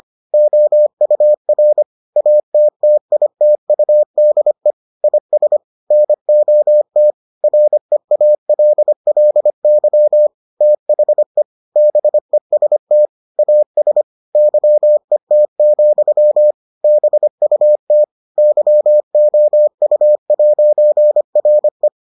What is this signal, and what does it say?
Morse code, heard as a keyed tone, about 25 wpm, 600 Hz. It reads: OUR ATTITUDE IS NOT REALLY THE BEST AS YET, BUT YOU'RE